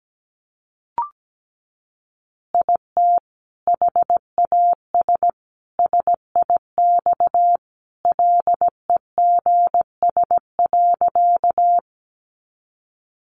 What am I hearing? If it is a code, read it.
Morse code, heard as a keyed tone, about 17 wpm, 700 Hz. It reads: IT HAS SIX LEGS.
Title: Additional Easy Phrases 17wpm